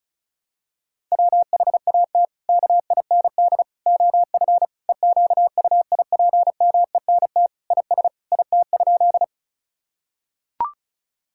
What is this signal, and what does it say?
Morse code, heard as a keyed tone, about 36 wpm, 700 Hz. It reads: WHAT KIND OF EQUIPMENT IS IT?
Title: Sentences from Top 1000 Words - Review Part 5 - 35wpm